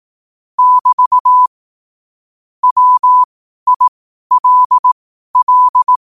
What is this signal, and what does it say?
Morse code, heard as a keyed tone, about 18 wpm, spaced as for 12 wpm, 1000 Hz. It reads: <BT> WILL
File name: qotdfiles / 12wpm